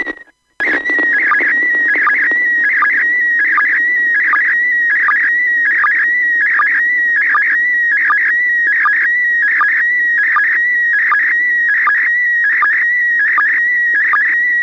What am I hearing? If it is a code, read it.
SSTV (slow-scan television), PD 180.
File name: iss sstv 145mhz800 19h40 utc